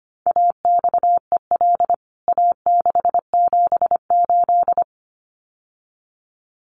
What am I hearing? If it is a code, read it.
Morse code, heard as a keyed tone, about 25 wpm, 700 Hz. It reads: AXEL A678
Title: CWT Exchange 25wpm